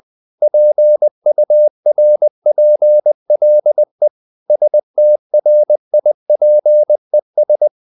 Morse code, heard as a keyed tone, about 20 wpm, 600 Hz. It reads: PURPLE STRIPES
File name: the old man and the sea part 2 20wpm